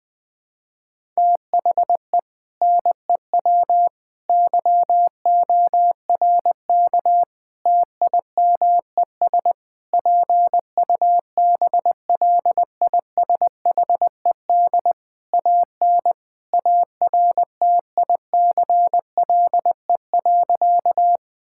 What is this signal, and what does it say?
Morse code, heard as a keyed tone, about 21 wpm, 700 Hz. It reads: THE NEW YORK TIMES PUBLISHED AN ARTICLE.
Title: Sentences with 4-Word Phrases - 20wpm